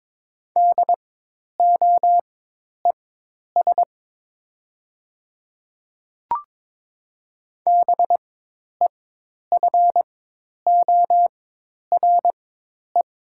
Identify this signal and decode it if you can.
Morse code, heard as a keyed tone, about 23 wpm, spaced as for 10 wpm, 700 Hz. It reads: DOES BEFORE
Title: 4x Character Spacing - Top 300 Words - 22wpm